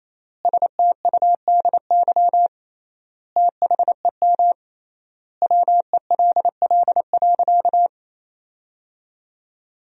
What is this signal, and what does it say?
Morse code, heard as a keyed tone, about 28 wpm, 700 Hz. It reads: STUDY THEM WELL.
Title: 3x Word-Spacing - Sentences from Top 300 Words - 28wpm